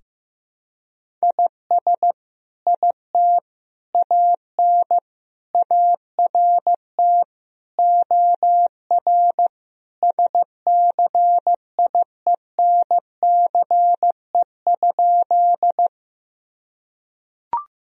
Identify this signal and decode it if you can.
Morse code, heard as a keyed tone, about 15 wpm, 700 Hz. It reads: IS IT AN ART OR SCIENCE?
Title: Sentences from Top 1000 Words - Review Part 3 - 15wpm